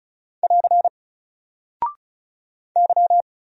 Morse code, heard as a keyed tone, about 36 wpm, 700 Hz. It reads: <AR> Y
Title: Single Character (Letter:Number:Punctuation:Prosign) 35wpm